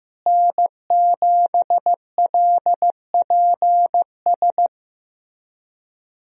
Morse code, heard as a keyed tone, about 15 wpm, 700 Hz. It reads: N7LPS
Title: USA Call Signs - 5 Characters - Repeated 3x - 15wpm